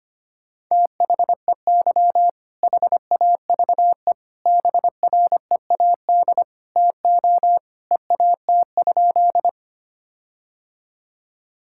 Morse code, heard as a keyed tone, about 25 wpm, 700 Hz. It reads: THEY HAVE BREAD TO EAT?
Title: Sentences from Top 1600 Words 25wpm